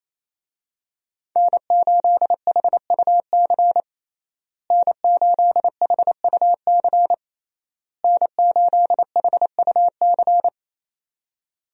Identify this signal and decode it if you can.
Morse code, heard as a keyed tone, about 28 wpm, 700 Hz. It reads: N8HUC N8HUC N8HUC
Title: USA Call Signs - 5 Characters - Repeated 3x - 28wpm